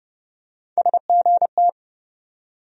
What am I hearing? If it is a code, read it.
Morse code, heard as a keyed tone, about 30 wpm, 700 Hz. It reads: SGT
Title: USA Call Sign Postfixes - 3 Letters 30wpm